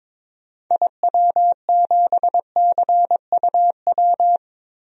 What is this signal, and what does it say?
Morse code, heard as a keyed tone, about 21 wpm, 700 Hz. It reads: IW7CUW
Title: DXCC-100 Call Signs - 6 Characters - 22wpm